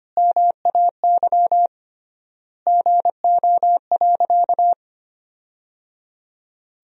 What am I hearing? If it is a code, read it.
Morse code, heard as a keyed tone, about 25 wpm, 700 Hz. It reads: MAY GO.
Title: 3x Word-Spacing - Sentences from Top 100 Words - 25wpm